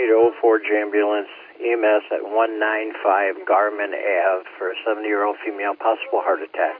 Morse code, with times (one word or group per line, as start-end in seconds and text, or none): none